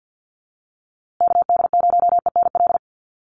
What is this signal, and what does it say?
Morse code, heard as a keyed tone, about 50 wpm, 700 Hz. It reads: KD0ENL
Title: USA Call Signs Repeated 3x 50wpm